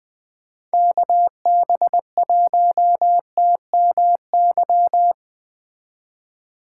Morse code, heard as a keyed tone, about 20 wpm, 700 Hz. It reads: KB1TMY
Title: Speed Racing - USA 2x3 Call Signs 17wpm